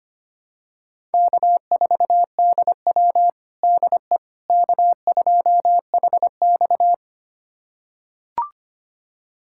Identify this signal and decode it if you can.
Morse code, heard as a keyed tone, about 25 wpm, 700 Hz. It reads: K4DW DE K2HX
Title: USA Callsign DE Callsign - 4 Characters - 25wpm